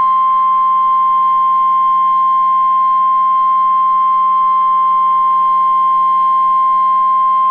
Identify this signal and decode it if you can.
Emergency Alert System (SAME) tones (the 1050 Hz attention tone). It is heard from 0.8 to 6.8 s.